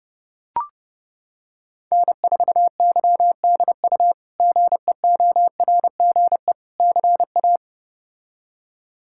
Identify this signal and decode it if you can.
Morse code, heard as a keyed tone, about 32 wpm, 700 Hz. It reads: N4YDU GEORGE CA
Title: K1USN SST QSO Practice - 30wpm